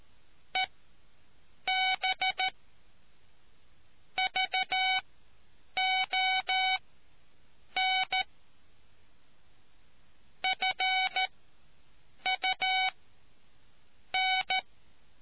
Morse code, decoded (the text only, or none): EB VON FUN